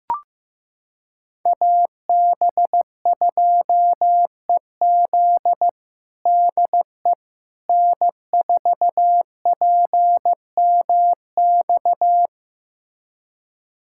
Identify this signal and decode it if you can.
Morse code, heard as a keyed tone, about 15 wpm, 700 Hz. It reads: AB2EZ DE N4PMX
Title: USA Callsign DE Callsign - 5 Characters - 15wpm